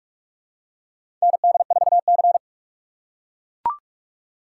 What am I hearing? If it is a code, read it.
Morse code, heard as a keyed tone, about 44 wpm, 700 Hz. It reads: ND4C
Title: USA Call Signs - 4 Characters - 6x Character Spacing - 45wpm